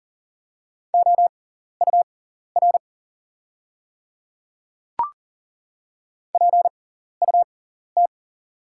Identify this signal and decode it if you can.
Morse code, heard as a keyed tone, about 40 wpm, spaced as for 14 wpm, 690 Hz. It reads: OUR PUT
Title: 6x Character Spacing - Top 500 Words - 40wpm